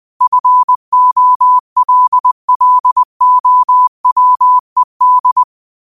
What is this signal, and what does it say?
Morse code, heard as a keyed tone, about 20 wpm, 1000 Hz. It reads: FOLLOWED
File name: qotdfiles / 20wpm